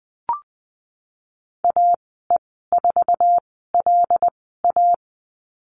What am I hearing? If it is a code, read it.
Morse code, heard as a keyed tone, about 19 wpm, spaced as for 14 wpm, 700 Hz. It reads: AE4LA
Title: 088 Lesson 15g - Number 5 - Call Signs 20wpm